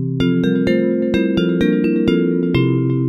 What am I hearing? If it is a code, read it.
There is no code.